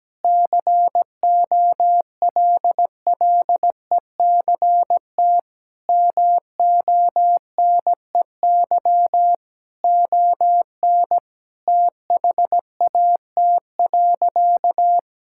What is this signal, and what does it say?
Morse code, heard as a keyed tone, about 17 wpm, 700 Hz. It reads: COLLECT MONEY ON THAT.
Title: Sentences from Top 1600 Words 17wpm